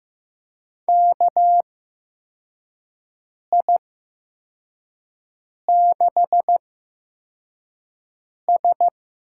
Morse code, heard as a keyed tone, about 15 wpm, spaced as for 4 wpm, 700 Hz. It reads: KI6S
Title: USA Call Signs - 6 Characters - 8x Character Spacing - 15wpm